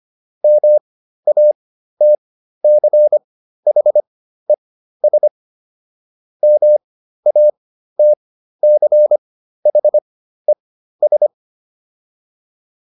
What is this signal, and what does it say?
Morse code, heard as a keyed tone, about 25 wpm, spaced as for 13 wpm, 600 Hz. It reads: MATCHES MATCHES